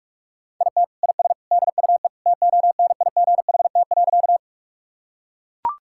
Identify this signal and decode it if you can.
Morse code, heard as a keyed tone, about 42 wpm, 700 Hz. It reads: IT IS DUE TONIGHT.
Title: Sentences from Top 1000 Words - Review Part 6 - 45wpm